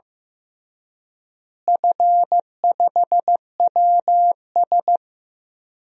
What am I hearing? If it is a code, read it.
Morse code, heard as a keyed tone, about 15 wpm, 700 Hz. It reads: F5WS
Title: DXCC-100 Call Signs - 4 Characters - Repeated 3x - 15wpm